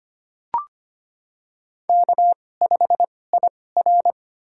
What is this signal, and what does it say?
Morse code, heard as a keyed tone, about 25 wpm, spaced as for 18 wpm, 700 Hz. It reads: K5IR
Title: USA Call Signs - 4 Characters - 2x Character Spacing - 25wpm